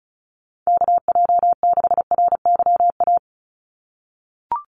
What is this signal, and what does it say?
Morse code, heard as a keyed tone, about 34 wpm, 700 Hz. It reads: KJ6RYA
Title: USA Call Signs 35wpm